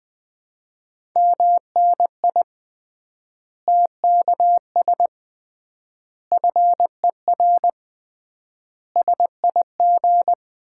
Morse code, heard as a keyed tone, about 21 wpm, 700 Hz. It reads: MNI TKS FER SIG
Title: Common QSO Phrases - 3x Word-Spacing - 20wpm